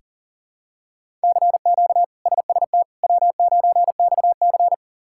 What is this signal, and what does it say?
Morse code, heard as a keyed tone, about 40 wpm, 700 Hz. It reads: CQ SST W9XC
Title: K1USN SST QSO Practice - 40wpm